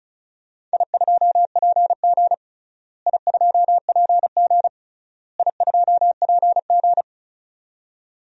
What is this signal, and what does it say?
Morse code, heard as a keyed tone, about 34 wpm, 700 Hz. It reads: I2PG I2PG I2PG
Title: DXCC-100 Call Signs - 4 Characters - Repeated 3x - 35wpm